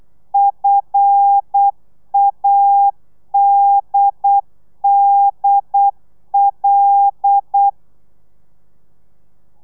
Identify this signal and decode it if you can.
Morse code, heard as a keyed tone, about 8 wpm, 800 Hz. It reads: FADDL
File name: telegrafia / i2tfj / Lez5 35